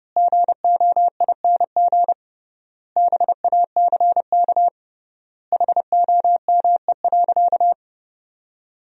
Morse code, heard as a keyed tone, about 30 wpm, 700 Hz. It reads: GOING BACK HOME.
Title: 3x Word-Spacing - Sentences from Top 500 Words - 30wpm